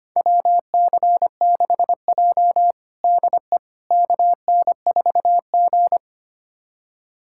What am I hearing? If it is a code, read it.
Morse code, heard as a keyed tone, about 25 wpm, 700 Hz. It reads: WC6J DE KN4G
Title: USA Callsign DE Callsign - 4 Characters - 25wpm